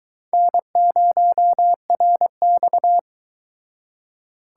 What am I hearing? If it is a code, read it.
Morse code, heard as a keyed tone, about 23 wpm, 700 Hz. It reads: N0RX